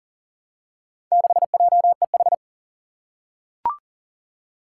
Morse code, heard as a keyed tone, about 40 wpm, 700 Hz. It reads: 6JEH
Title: Four Challenging Letter-Number Combinations 40wpm